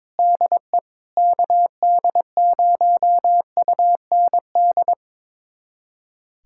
Morse code, heard as a keyed tone, about 21 wpm, 700 Hz. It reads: DE KD0UND
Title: USA Callsign DE Callsign - 6 Characters - 22wpm